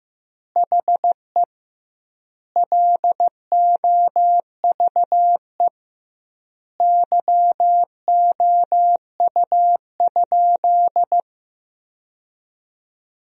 Morse code, heard as a keyed tone, about 16 wpm, 700 Hz. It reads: HE LOVE YOU?
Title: 2x Word-Spacing - Sentences from Top 400 Words - 15wpm